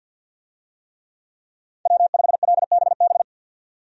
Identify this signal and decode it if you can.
Morse code, heard as a keyed tone, about 47 wpm, 700 Hz. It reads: W5LBB
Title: USA Call Signs - 5 Characters - 50wpm